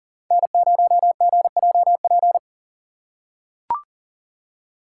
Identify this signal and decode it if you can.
Morse code, heard as a keyed tone, about 39 wpm, 700 Hz. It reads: N0GJP
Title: USA Call Signs - 2x Character Spacing - 40wpm